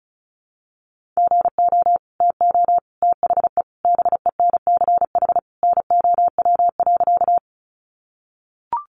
Morse code, heard as a keyed tone, about 34 wpm, 700 Hz. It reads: GO TO THE BENCH NOW.
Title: Sentences from Top 1400 Words 35wpm